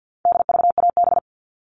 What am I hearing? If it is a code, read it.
Morse code, heard as a keyed tone, about 50 wpm, 700 Hz. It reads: DVAB
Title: All Cut Numbers - 4-Digit 50wpm